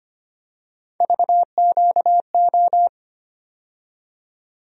Morse code, heard as a keyed tone, about 25 wpm, 700 Hz. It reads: VQO
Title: USA Call Sign Postfixes - 3 Letters 25wpm